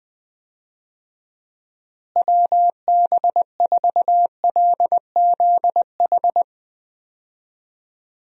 Morse code, heard as a keyed tone, about 20 wpm, 700 Hz. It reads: WB4LZH
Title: USA Call Signs - 6 Characters - Repeated 3x - 20wpm